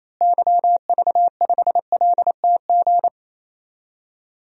Morse code, heard as a keyed tone, about 28 wpm, 700 Hz. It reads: YV5LTG
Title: DXCC-100 Call Signs - 6 Characters - Repeated 3x - 28wpm